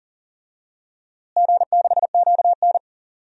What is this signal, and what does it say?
Morse code, heard as a keyed tone, about 40 wpm, 700 Hz. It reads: G6QN
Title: DXCC-100 Call Signs - 4 Characters - Repeated 3x - 40wpm